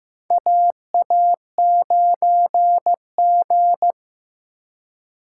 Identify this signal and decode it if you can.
Morse code, heard as a keyed tone, about 15 wpm, 700 Hz. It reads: AA9G
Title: USA Call Signs - 4 Characters - 4x Character Spacing - 15wpm